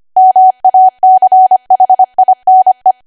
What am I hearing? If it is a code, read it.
Morse code, heard as a keyed tone, about 25 wpm, 750 Hz. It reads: MACHINE
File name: Archive / 25 WPM / 25WPM